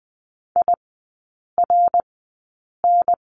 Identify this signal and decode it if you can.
Morse code, heard as a keyed tone, about 20 wpm, spaced as for 8 wpm, 700 Hz. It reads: IRN
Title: 053 Lesson 10e - Letter r - Three Characters 20wpm